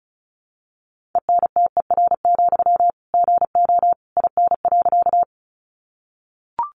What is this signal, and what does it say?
Morse code, heard as a keyed tone, about 34 wpm, 700 Hz. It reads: ENTER, GO IN.
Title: Sentences from Top 1300 Words 35wpm